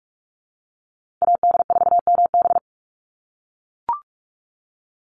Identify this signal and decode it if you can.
Morse code, heard as a keyed tone, about 44 wpm, 700 Hz. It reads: AD4MB
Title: USA Call Signs 45wpm